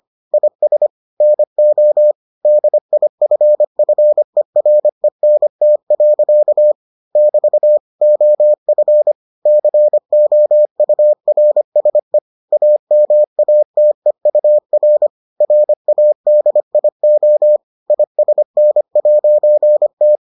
Morse code, heard as a keyed tone, about 25 wpm, 600 Hz. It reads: IS NO DIFFERENT. <BT> OF COURSE AMATEUR RADIO ISN'T